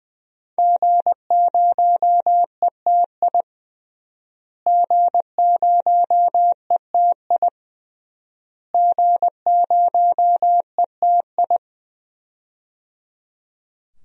Morse code, heard as a keyed tone, about 21 wpm, 700 Hz. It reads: G0ETI G0ETI G0ETI
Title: DXCC-100 Call Signs - 5 Characters - Repeated 3x - 20wpm